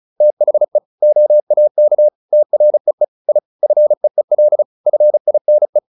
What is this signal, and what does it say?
Morse code, heard as a keyed tone, about 34 wpm, 600 Hz. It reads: THE OAK TREE I FEEL FINE